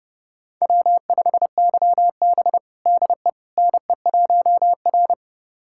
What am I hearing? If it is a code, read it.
Morse code, heard as a keyed tone, about 32 wpm, 700 Hz. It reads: W5YB DE NE1R
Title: USA Callsign DE Callsign - 4 Characters - 30wpm